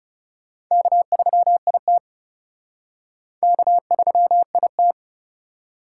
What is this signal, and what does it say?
Morse code, heard as a keyed tone, about 36 wpm, 700 Hz. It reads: K3IT K3IT
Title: Speed Racing - USA 1x2 Call Signs 30wpm